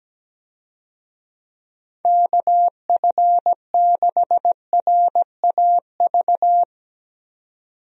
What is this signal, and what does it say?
Morse code, heard as a keyed tone, about 17 wpm, 700 Hz. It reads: KF6RAV
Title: USA Call Signs - 6 Characters - 17wpm